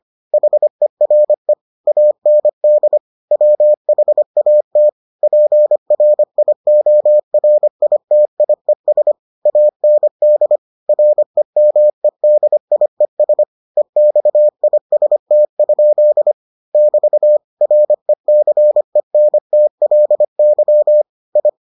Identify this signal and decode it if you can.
Morse code, heard as a keyed tone, about 25 wpm, 600 Hz. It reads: HERE AND WHAT PRIORITIES AND REMEDIES EXIST? <BT> RECENTLY I